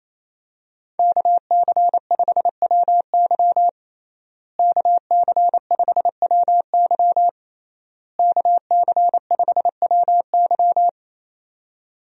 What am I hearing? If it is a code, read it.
Morse code, heard as a keyed tone, about 27 wpm, 700 Hz. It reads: KC5WY KC5WY KC5WY
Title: USA Call Signs - 5 Characters - Repeated 3x - 28wpm